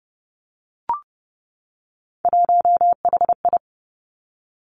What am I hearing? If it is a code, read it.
Morse code, heard as a keyed tone, about 30 wpm, 700 Hz. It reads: 1HI
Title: Three Challenging Letter-Number Combinations 30wpm